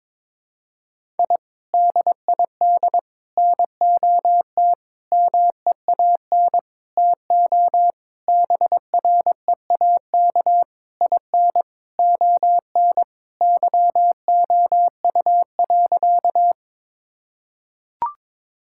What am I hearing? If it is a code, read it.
Morse code, heard as a keyed tone, about 22 wpm, 700 Hz. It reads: I DID NOT MEAN TO BREAK IN ON YOU.
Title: Sentences from Top 1000 Words - Review Part 6 - 22wpm